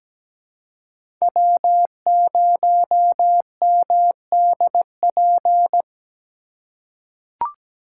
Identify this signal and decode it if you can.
Morse code, heard as a keyed tone, about 17 wpm, 700 Hz. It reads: W0MDP
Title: USA Call Signs - 5 Characters - 3x Character Spacing - 17wpm